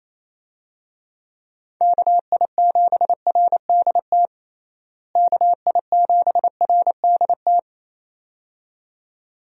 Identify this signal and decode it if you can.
Morse code, heard as a keyed tone, about 28 wpm, 700 Hz. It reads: KI7RDT KI7RDT
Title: USA Call Signs - 6 Characters - Repeated 2x - 28wpm